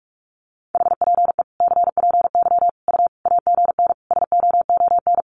Morse code, heard as a keyed tone, about 44 wpm, 700 Hz. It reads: HPE CPY U AGN SOON
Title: Common QSO Phrases 45wpm